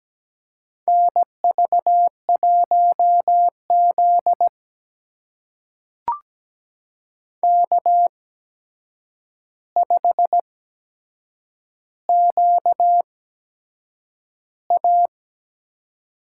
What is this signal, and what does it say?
Morse code, heard as a keyed tone, about 17 wpm, spaced as for 7 wpm, 700 Hz. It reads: NV1Z K5QA
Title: USA Call Signs - 8x Character Spacing - 17wpm